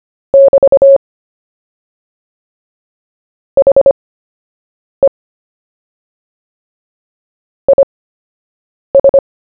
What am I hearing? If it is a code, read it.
Morse code, heard as a keyed tone, about 25 wpm, spaced as for 7 wpm, 550 Hz. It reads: <BT> HE IS
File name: cw / fists / Short-Story-1-7wpm